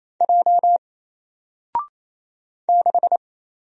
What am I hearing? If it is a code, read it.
Morse code, heard as a keyed tone, about 28 wpm, 700 Hz. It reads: J 6
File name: Single-Letter-Number-28wpm